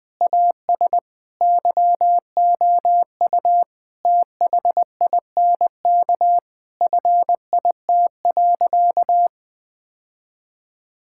Morse code, heard as a keyed tone, about 21 wpm, 700 Hz. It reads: AS YOU THINK FIT.
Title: Sentences from Top 1000 Words - Review Part 5 - 20wpm